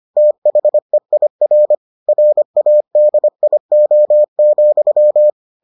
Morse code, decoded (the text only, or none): THEIR RADIO,